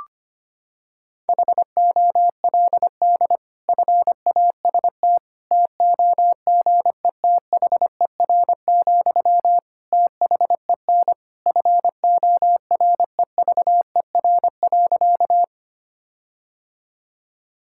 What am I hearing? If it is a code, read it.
Morse code, heard as a keyed tone, about 25 wpm, 700 Hz. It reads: HOLD FAST TOGETHER, THEN FOREVER.
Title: Sentences from Top 1600 Words 25wpm